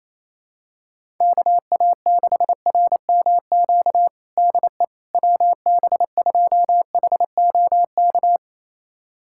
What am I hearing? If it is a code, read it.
Morse code, heard as a keyed tone, about 27 wpm, 700 Hz. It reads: KA6RMQ DE WB2HOK
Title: USA Callsign DE Callsign - 6 Characters - 28wpm